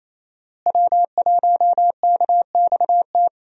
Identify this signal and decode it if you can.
Morse code, heard as a keyed tone, about 28 wpm, 700 Hz. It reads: W1KXT
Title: USA Call Signs - 5 Characters - Repeated 2x - 28wpm